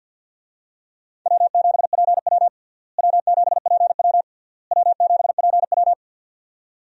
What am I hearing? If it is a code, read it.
Morse code, heard as a keyed tone, about 49 wpm, 700 Hz. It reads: W7PW W7PW W7PW
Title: USA Call Signs - 4 Characters - Repeated 3x - 50wpm